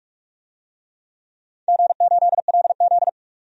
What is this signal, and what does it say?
Morse code, heard as a keyed tone, about 43 wpm, 700 Hz. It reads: G8LZ
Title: DXCC-100 Call Signs - 4 Characters - 45wpm